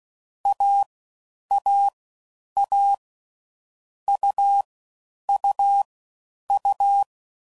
Morse code, heard as a keyed tone, about 16 wpm, 800 Hz. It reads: A A A U U U
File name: Combinations / Difficult Combos / AUV Short